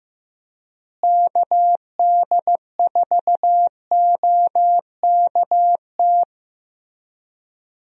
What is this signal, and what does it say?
Morse code, heard as a keyed tone, about 15 wpm, 700 Hz. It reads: KD4OKT
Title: USA Call Signs - 6 Characters - Repeated 3x - 15wpm